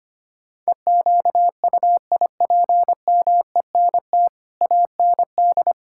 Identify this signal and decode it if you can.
Morse code, heard as a keyed tone, about 25 wpm, 700 Hz. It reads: EQUIPMENT AND
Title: Sentences from Top 1000 Words - Review Part 5 - 25wpm